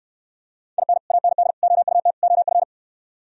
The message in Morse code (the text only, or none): IN AND OUT OF